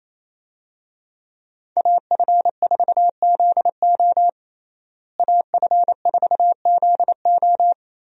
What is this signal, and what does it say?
Morse code, heard as a keyed tone, about 27 wpm, 700 Hz. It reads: AF4ZO AF4ZO
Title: USA Call Signs - 5 Characters - Repeated 3x - 28wpm